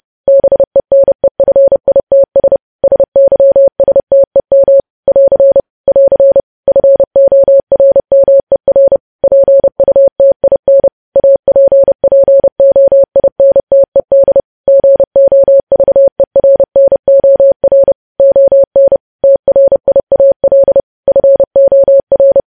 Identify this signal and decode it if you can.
Morse code, heard as a keyed tone, about 30 wpm, 550 Hz. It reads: BENEFITS SYSTEM <AR> <AR> FORMER PUTIN APPOINTED GOVERNOR ON TRIAL FOR
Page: news headlines in morse code at 30 wpm